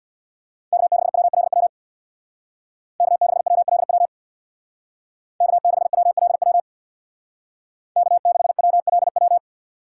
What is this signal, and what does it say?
Morse code, heard as a keyed tone, about 64 wpm, 700 Hz. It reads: K6WLW K6WLW K6WLW K6WLW